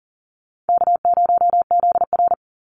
Morse code, heard as a keyed tone, about 40 wpm, 700 Hz. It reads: K0ZR
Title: Easy Call Signs - 40wpm